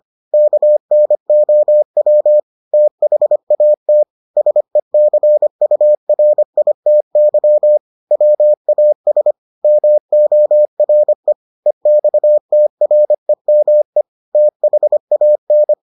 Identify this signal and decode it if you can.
Morse code, heard as a keyed tone, about 25 wpm, 600 Hz. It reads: KNOW THAT SECURITY WAS MORE EXTREME THAN